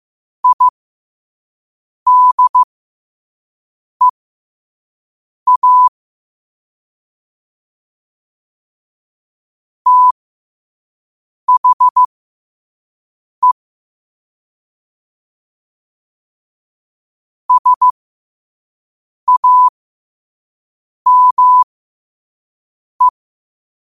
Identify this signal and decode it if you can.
Morse code, heard as a keyed tone, about 15 wpm, spaced as for 5 wpm, 1000 Hz. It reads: IDEA THE SAME